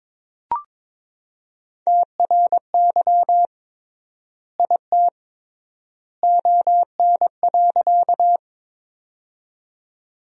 Morse code, heard as a keyed tone, about 23 wpm, 700 Hz. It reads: TRY IT ON.
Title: 3x Word-Spacing - Sentences from Top 200 Words - 22wpm